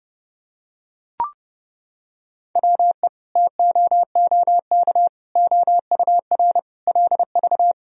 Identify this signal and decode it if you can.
Morse code, heard as a keyed tone, about 30 wpm, 700 Hz. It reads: WE TOOK OUR LV
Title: Sentences from Top 400 Words and Abbreviations 30wpm